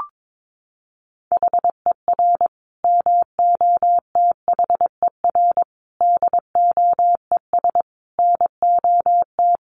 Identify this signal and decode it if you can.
Morse code, heard as a keyed tone, about 22 wpm, 700 Hz. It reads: HER MOTHER DOES NOT
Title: Sentences from Top 1400 Words 22wpm